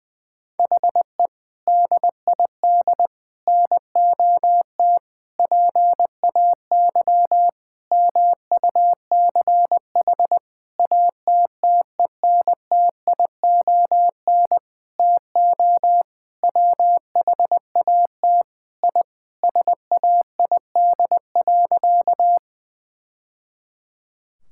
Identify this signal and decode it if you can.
Morse code, heard as a keyed tone, about 20 wpm, 700 Hz. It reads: HE DID NOT PAY MUCH ATTENTION TO WHAT I SAID.
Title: Sentences from Top 1000 Words - Review Part 6 - 20wpm